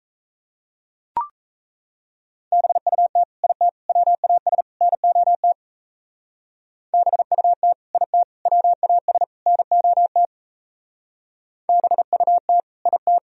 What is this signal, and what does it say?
Morse code, heard as a keyed tone, about 39 wpm, 700 Hz. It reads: BUT IT WAS NOT BUT IT WAS NOT BUT IT